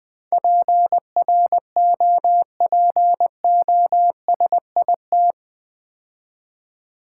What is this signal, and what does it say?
Morse code, heard as a keyed tone, about 20 wpm, 700 Hz. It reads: PROPOSIT
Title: Build A Word - 4 Letters At A Time - 11-Letter Words - 20wpm